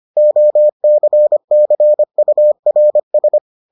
Morse code, heard as a keyed tone, about 25 wpm, 600 Hz. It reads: OCCURS